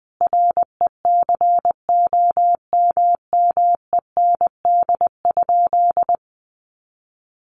Morse code, heard as a keyed tone, about 20 wpm, 700 Hz. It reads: RECOMMEND?
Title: Sentences from Top 1200 Words 20wpm